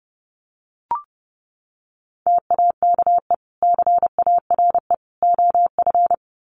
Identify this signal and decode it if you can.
Morse code, heard as a keyed tone, about 30 wpm, 700 Hz. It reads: TAKE CARE OF
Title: Sets of 3 Words 30wpm